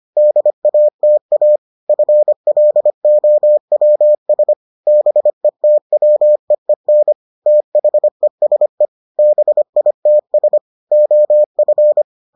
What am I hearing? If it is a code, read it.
Morse code, heard as a keyed tone, about 25 wpm, 600 Hz. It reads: DATA FLOWS BETWEEN THESE BITS OF